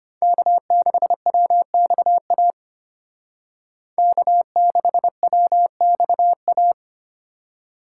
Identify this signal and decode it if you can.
Morse code, heard as a keyed tone, about 30 wpm, 700 Hz. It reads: K6WXA K6WXA
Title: Speed Racing - USA 1x3 Call Signs 25wpm